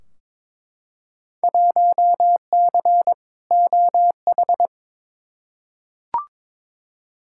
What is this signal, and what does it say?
Morse code, heard as a keyed tone, about 22 wpm, 700 Hz. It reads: 1C OH
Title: Field Day Exchanges - 22wpm